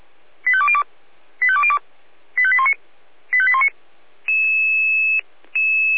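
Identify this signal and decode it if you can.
Five-tone selective call (ZVEI), heard as quick runs of five tones, tones of about 70 ms. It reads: 85202 85202 85819 85819
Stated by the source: ZVEI five-tone selective call 85101 received on an SDR.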